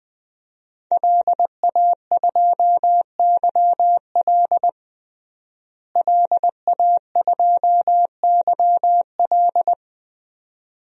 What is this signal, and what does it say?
Morse code, heard as a keyed tone, about 20 wpm, 700 Hz. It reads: LA2YL LA2YL
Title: DXCC-100 Call Signs - 5 Characters - Repeated 2x - 20wpm